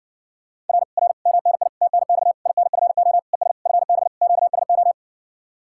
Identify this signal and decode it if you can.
Morse code, heard as a keyed tone, about 60 wpm, 700 Hz. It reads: R R MNI TNX INFO ES FB QSO